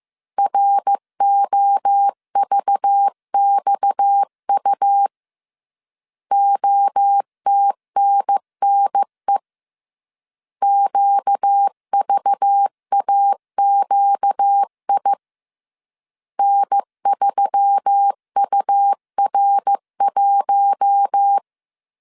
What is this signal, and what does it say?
Morse code, heard as a keyed tone, about 15 wpm, 800 Hz. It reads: ROVXU OTNNE QVAQI N3UR1
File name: CW / DET / 1 6 1500C S B 14wpm 2958